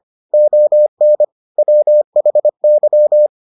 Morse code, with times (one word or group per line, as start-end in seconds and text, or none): 0.3-1.2 ON
1.6-3.3 WHY